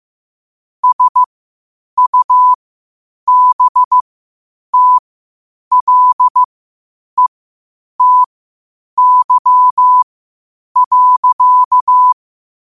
Morse code, heard as a keyed tone, about 15 wpm, spaced as for 8 wpm, 1000 Hz. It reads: SUBTLETY.
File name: qotdfiles / 8wpm